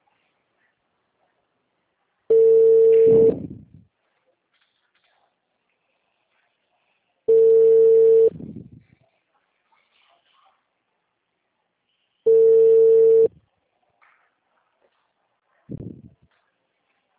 Ringback tone (China, 450 Hz).